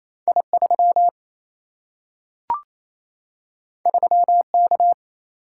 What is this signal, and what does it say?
Morse code, heard as a keyed tone, about 28 wpm, 700 Hz. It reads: I3 3K
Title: Single-Letter Single-Number 28wpm